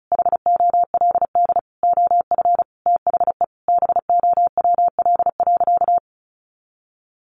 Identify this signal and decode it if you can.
Morse code, heard as a keyed tone, about 34 wpm, 700 Hz. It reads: HOLD OF THE BOWL.
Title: Sentences from Top 1200 Words 35wpm